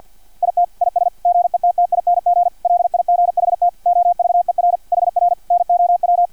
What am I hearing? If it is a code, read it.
Morse code, heard as a keyed tone, about 50 wpm, 700 Hz, with radio noise. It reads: IT IS GETTING LIGHT OVER HR NOW
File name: archived / CRSnet